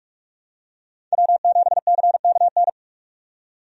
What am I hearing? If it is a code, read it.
Morse code, heard as a keyed tone, about 46 wpm, 700 Hz. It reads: W7CKN
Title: USA Call Signs - 5 Characters - 45wpm